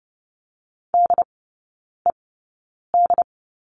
Morse code, heard as a keyed tone, about 30 wpm, spaced as for 9 wpm, 700 Hz. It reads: DED
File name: course / 30:10wpm / 087 Lesson 15e - Number 5 - Three Characters 30wpm